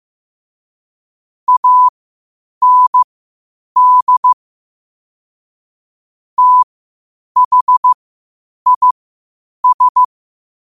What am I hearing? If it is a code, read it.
Morse code, heard as a keyed tone, about 15 wpm, spaced as for 8 wpm, 1000 Hz. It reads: AND THIS
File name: qotdfiles / 8wpm